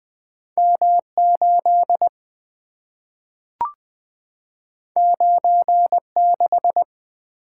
Morse code, heard as a keyed tone, about 20 wpm, 700 Hz. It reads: M 8 9 6
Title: Two Letter-Number Combinations - 20wpm